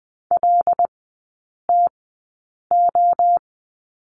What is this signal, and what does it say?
Morse code, heard as a keyed tone, about 20 wpm, spaced as for 8 wpm, 690 Hz. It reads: LTO
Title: 093 Lesson 16e - Letter c - Three Characters 20wpm